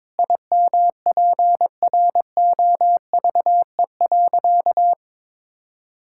Morse code, heard as a keyed tone, about 22 wpm, 700 Hz. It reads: IMPROVE.
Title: Sentences from Top 1000 Words - Review Part 5 - 22wpm